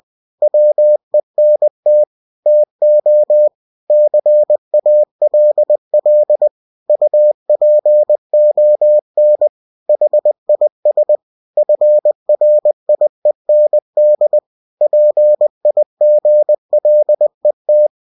Morse code, heard as a keyed tone, about 20 wpm, 600 Hz. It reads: WENT TO CALL UPON HIS FRIEND PIGLET